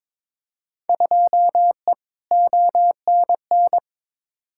Morse code, heard as a keyed tone, about 22 wpm, 700 Hz. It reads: 2E ONN
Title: Field Day Exchanges Encore - 22wpm